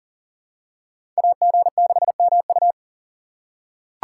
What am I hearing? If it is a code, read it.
Morse code, heard as a keyed tone, about 40 wpm, 700 Hz. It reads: AG6MU
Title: USA Call Signs - 5 Characters - 4x Character Spacing - 40wpm